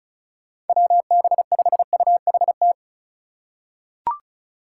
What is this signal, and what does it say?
Morse code, heard as a keyed tone, about 34 wpm, 700 Hz. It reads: WB5UHT
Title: USA Call Signs - 6 Characters - 3x Character Spacing - 35wpm